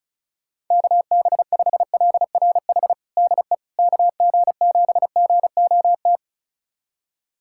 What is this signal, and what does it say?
Morse code, heard as a keyed tone, about 36 wpm, 690 Hz. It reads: KB5LRH DE KG7GOT
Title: USA Callsign DE Callsign 35wpm